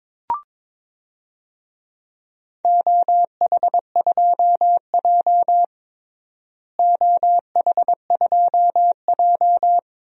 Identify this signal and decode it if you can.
Morse code, heard as a keyed tone, about 22 wpm, 700 Hz. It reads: OH2J OH2J
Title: DXCC-100 Call Signs - 4 Characters - Repeated 3x - 22wpm